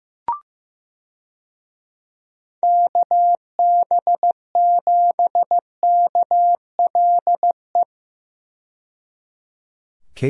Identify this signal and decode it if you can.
Morse code, heard as a keyed tone, about 15 wpm, 700 Hz. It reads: KB7KLE
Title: USA Call Signs - 6 Characters - 15wpm